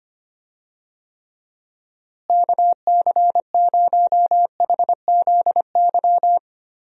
Morse code, heard as a keyed tone, about 25 wpm, 700 Hz. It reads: KC0HZY
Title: USA Call Signs - 6 Characters - Repeated 2x - 25wpm